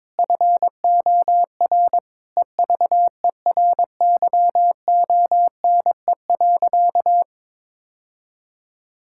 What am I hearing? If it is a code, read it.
Morse code, heard as a keyed tone, about 21 wpm, 700 Hz. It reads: FOR EVERYONE.
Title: Sentences from Top 1000 Words - Review Part 5 - 22wpm